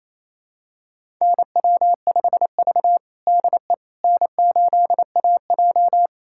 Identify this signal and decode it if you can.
Morse code, heard as a keyed tone, about 28 wpm, 700 Hz. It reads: NW5V DE N8AJ
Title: USA Callsign DE Callsign - 4 Characters - 28wpm